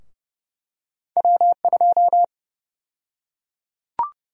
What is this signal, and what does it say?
Morse code, heard as a keyed tone, about 30 wpm, 700 Hz. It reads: W2
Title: USA Call Sign Prefixes - 30wpm